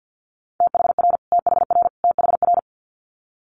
Morse code, heard as a keyed tone, about 50 wpm, 700 Hz. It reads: THR THR THR
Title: Multi-letter Phonemes Repeated 3x 50wpm